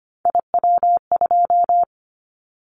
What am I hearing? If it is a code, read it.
Morse code, heard as a keyed tone, about 25 wpm, 700 Hz. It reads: IW2
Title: Callsign Prefixes for Top 25 DXCC - 25wpm